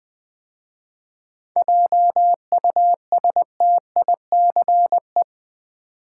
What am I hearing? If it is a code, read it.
Morse code, heard as a keyed tone, about 20 wpm, 700 Hz. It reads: JUSTICE
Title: 7 Letter Words - First 3 Letters Given - 20wpm